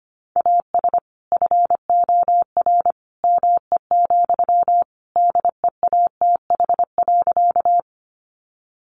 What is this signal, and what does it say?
Morse code, heard as a keyed tone, about 25 wpm, 700 Hz. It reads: AS FOR ME, DEATH.
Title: Sentences from Top 1100 Words 25wpm